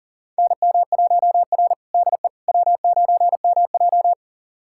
Morse code, heard as a keyed tone, about 40 wpm, 700 Hz. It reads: NM1R DE W9MJ
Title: USA Callsign DE Callsign - 4 Characters - 40wpm